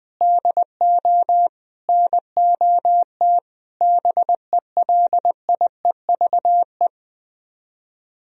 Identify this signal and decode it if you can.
Morse code, heard as a keyed tone, about 20 wpm, 700 Hz. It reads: DO NOT BELIEVE
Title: Speed-Racing - Sets of 4 Words 20wpm